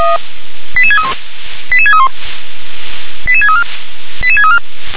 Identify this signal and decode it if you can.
Five-tone selective call (ZVEI), heard as quick runs of five tones, tones of about 70 ms. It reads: 88631 88631 88634 88634